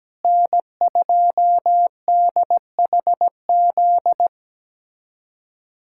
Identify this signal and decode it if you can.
Morse code, heard as a keyed tone, about 17 wpm, 700 Hz. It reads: N2DHZ
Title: USA Callsign DE Callsign - 5 Characters - 17wpm